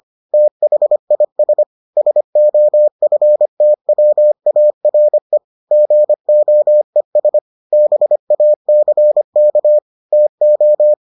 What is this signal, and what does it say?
Morse code, heard as a keyed tone, about 25 wpm, 600 Hz. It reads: THIS SOFTWARE GOES BACK TO